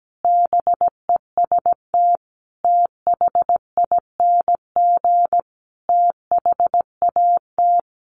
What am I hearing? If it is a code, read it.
Morse code, heard as a keyed tone, about 17 wpm, 700 Hz. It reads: BEST THING THAT
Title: Sentences from Top 400 Words 17wpm